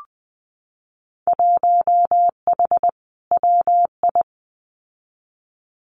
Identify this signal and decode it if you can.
Morse code, heard as a keyed tone, about 20 wpm, 700 Hz. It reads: 1H WI
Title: Winter Field Day Exchanges 20wpm